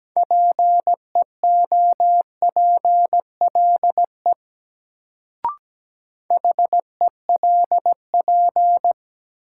Morse code, heard as a keyed tone, about 17 wpm, 700 Hz. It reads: PEOPLE HELP
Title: Sentences from Top 200 Words - New Word Drill - 17wpm